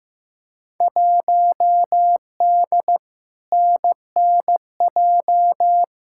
Morse code, heard as a keyed tone, about 15 wpm, 700 Hz. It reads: 1D NNJ
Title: Field Day Exchanges - 15wpm